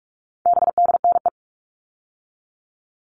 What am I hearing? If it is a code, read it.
Morse code, heard as a keyed tone, about 44 wpm, 700 Hz. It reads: BDNE
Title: All Cut Numbers - 4-Digit 45wpm